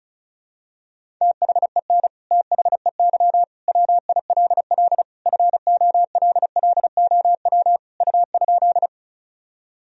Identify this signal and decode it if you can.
Morse code, heard as a keyed tone, about 36 wpm, 700 Hz. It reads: THEN THEY WILL FOLLOW U?
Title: Sentences from Top 200 Words and Abbreviations 35wpm